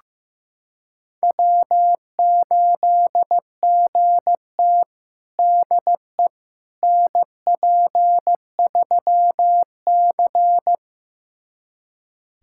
Morse code, heard as a keyed tone, about 16 wpm, 700 Hz. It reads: W8GT DE NP3C
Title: USA Callsign DE Callsign - 4 Characters - 15wpm